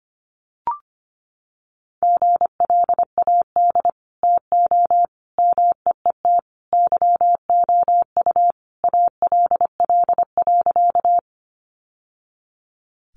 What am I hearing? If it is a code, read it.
Morse code, heard as a keyed tone, about 25 wpm, 700 Hz. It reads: GLAD TO MEET YOU ALL.
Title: Sentences from Top 1100 Words 25wpm